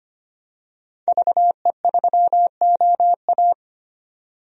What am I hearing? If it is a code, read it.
Morse code, heard as a keyed tone, about 25 wpm, 700 Hz. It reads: VE3OA
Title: DXCC-100 Call Signs - 5 Characters - 25wpm